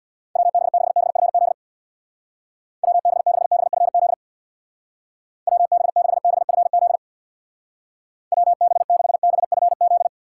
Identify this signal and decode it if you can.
Morse code, heard as a keyed tone, about 66 wpm, 700 Hz. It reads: WB6BFZ WB6BFZ WB6BFZ WB6BFZ